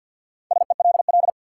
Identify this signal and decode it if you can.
Morse code, heard as a keyed tone, about 50 wpm, 700 Hz. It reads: SELL
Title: Sentences from Top 1100 Words - New Word Drill Alt - 50wpm